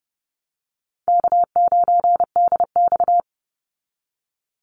Morse code, heard as a keyed tone, about 30 wpm, 700 Hz. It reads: K9DX
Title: Call Signs Repeated 3x - 30wpm